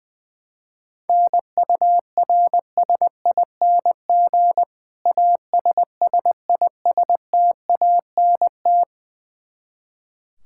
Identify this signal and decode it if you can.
Morse code, heard as a keyed tone, about 20 wpm, 700 Hz. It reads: NURSING ASSISTANT
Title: Practice QSO - Exchange 3 - More Info - 20wpm